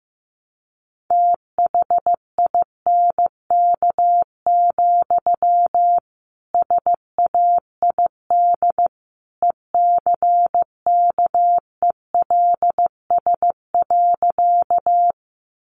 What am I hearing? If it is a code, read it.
Morse code, heard as a keyed tone, about 15 wpm, 700 Hz. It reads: THINK, SAID ECKELS.
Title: A Sound Of Thunder 15wpm